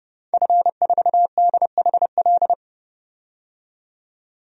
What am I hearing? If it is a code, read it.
Morse code, heard as a keyed tone, about 30 wpm, 700 Hz. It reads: F4DHL
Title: DXCC-100 Call Signs - 5 Characters - 30wpm